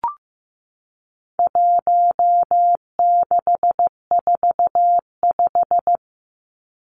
Morse code, heard as a keyed tone, about 15 wpm, 700 Hz. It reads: 1645
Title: Numbers - 4-Digit 15wpm